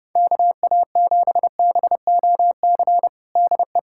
Morse code, heard as a keyed tone, about 30 wpm, 700 Hz. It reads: KA7BOC DE
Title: USA Callsign DE Callsign - 6 Characters - 30wpm